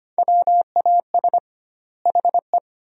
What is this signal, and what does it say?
Morse code, heard as a keyed tone, about 25 wpm, 700 Hz. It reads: WAS HE
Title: 2x Word-Spacing - Sentences from Top 400 Words - 25wpm